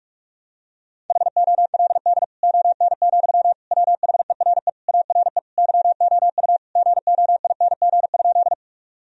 Morse code, heard as a keyed tone, about 41 wpm, 700 Hz. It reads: HOLD ON, WHERE ARE YOU GOING?
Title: Sentences from Top 1000 Words - Review Part 2 - 45wpm